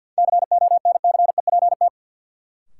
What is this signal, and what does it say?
Morse code, heard as a keyed tone, about 53 wpm, 700 Hz. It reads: CONCEPT
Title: Sentences from Top 1100 Words - New Word Drill - 50wpm